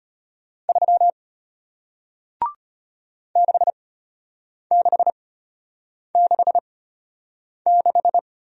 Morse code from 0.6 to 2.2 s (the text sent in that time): 3